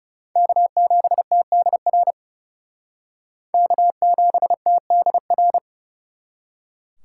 Morse code, heard as a keyed tone, about 36 wpm, 700 Hz. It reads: K7TDR K7TDR
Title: Speed Racing - USA 1x3 Call Signs 30wpm